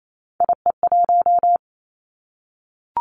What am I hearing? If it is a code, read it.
Morse code, heard as a keyed tone, about 28 wpm, 700 Hz. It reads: IE1
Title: Three Challenging Letter-Number Combinations 28wpm